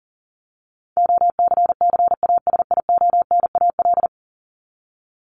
Morse code, heard as a keyed tone, about 40 wpm, 700 Hz. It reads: OCCASIONAL
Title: LY Suffix - Top 100 Words with Base Words 40wpm